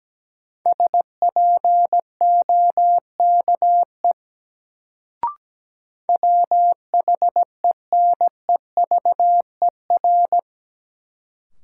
Morse code, heard as a keyed tone, about 17 wpm, 700 Hz. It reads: SPOKE WHENEVER
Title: Sentences from Top 1200 Words - New Word Drill Alt - 17wpm